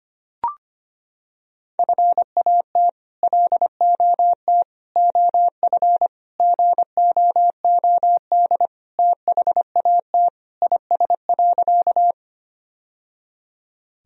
Morse code, heard as a keyed tone, about 25 wpm, 700 Hz. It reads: FAT LOT OF GOOD THAT IS.
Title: Sentences from Top 1000 Words - Review Part 4 - 25wpm